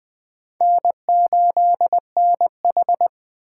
Morse code, heard as a keyed tone, about 25 wpm, 700 Hz. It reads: N8NH
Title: Speed Racing - USA 1x2 Call Signs 20wpm